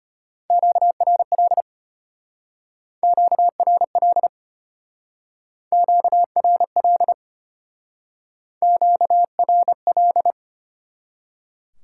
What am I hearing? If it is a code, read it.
Morse code, heard as a keyed tone, about 30 wpm, 700 Hz. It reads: QRL QRL QRL QRL
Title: Speed-Racing Top 90 QSO Elements 25wpm